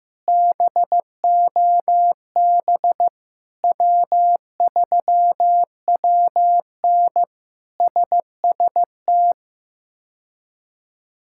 Morse code, heard as a keyed tone, about 15 wpm, 700 Hz. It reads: BOB W3WN SST
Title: K1USN SST QSO Practice - 15wpm